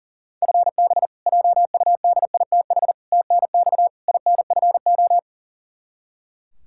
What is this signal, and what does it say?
Morse code, heard as a keyed tone, about 40 wpm, 700 Hz. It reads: FB JUDITH TNX INFO
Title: K1USN SST QSO Practice 40wpm